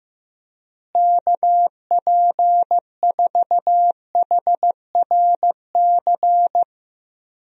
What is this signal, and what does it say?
Morse code, heard as a keyed tone, about 15 wpm, 700 Hz. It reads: KP4HRC
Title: USA Call Signs - 6 Characters - Repeated 2x - 15wpm